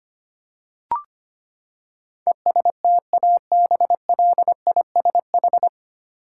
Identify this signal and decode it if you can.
Morse code, heard as a keyed tone, about 25 wpm, 700 Hz. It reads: ESTABLISH
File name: Nine Letter Words - 25wpm